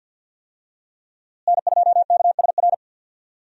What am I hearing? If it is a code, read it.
Morse code, heard as a keyed tone, about 52 wpm, 700 Hz. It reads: N2KSR
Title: USA Call Signs - 5 Characters - Repeated 3x - 50wpm